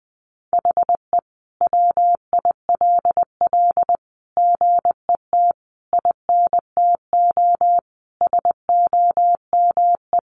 Morse code, heard as a keyed tone, about 20 wpm, 700 Hz. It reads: HE WILL GET INTO SOME
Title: Sentences from Top 1500 Words 20wpm